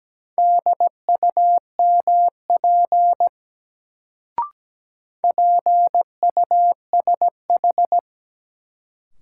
Morse code, heard as a keyed tone, about 17 wpm, 700 Hz. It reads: DUMP PUSH
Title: Sentences from Top 1300 Words - New Word Drill Alt - 17wpm